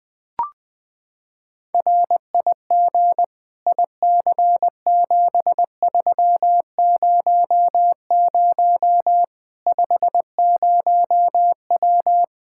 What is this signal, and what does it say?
Morse code, heard as a keyed tone, about 20 wpm, 700 Hz. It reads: RIG IC7300 50W
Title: Practice QSO - Exchange 2 - Info - 20wpm